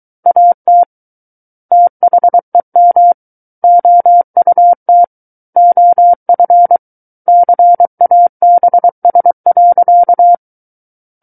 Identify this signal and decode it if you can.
Morse code, heard as a keyed tone, about 23 wpm, 700 Hz. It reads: AT THEM OUT OF CABS.
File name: Book One / 14-Chapter 14 - 23 WPM